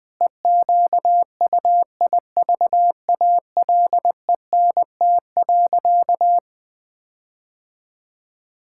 Morse code, heard as a keyed tone, about 21 wpm, 700 Hz. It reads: EQUIVALENT.
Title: Sentences from Top 1600 Words 20wpm